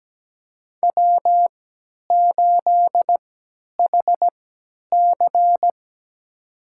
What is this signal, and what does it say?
Morse code, heard as a keyed tone, about 17 wpm, spaced as for 10 wpm, 700 Hz. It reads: W8HC
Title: USA Call Signs - 4 Characters - 3x Character Spacing - 17wpm